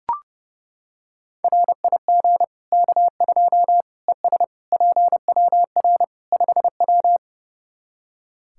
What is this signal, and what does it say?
Morse code, heard as a keyed tone, about 30 wpm, 700 Hz. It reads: RIG K2 ES PWR 5W
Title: Practice QSO - Exchange 2 - Info - 30wpm